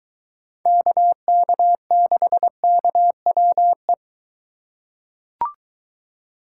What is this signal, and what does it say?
Morse code, heard as a keyed tone, about 22 wpm, 700 Hz. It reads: KK6KWE